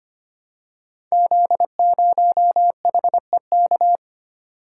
Morse code, heard as a keyed tone, about 25 wpm, 700 Hz. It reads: Z0HEK
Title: Code Groups - 5 Letter:Numbers - 25wpm